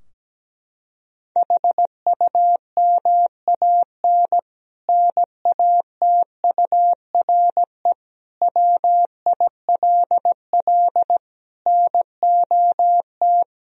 Morse code, heard as a keyed tone, about 17 wpm, 700 Hz. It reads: HUMAN NATURE WILL NOT
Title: Sentences from Top 1000 Words - Review Part 3 - 17wpm